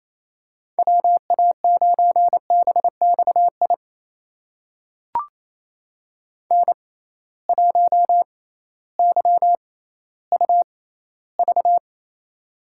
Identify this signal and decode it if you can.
Morse code, heard as a keyed tone, about 28 wpm, spaced as for 15 wpm, 700 Hz. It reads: WA9BXI N1YUV
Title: USA Call Signs - 6x Character Spacing - 28wpm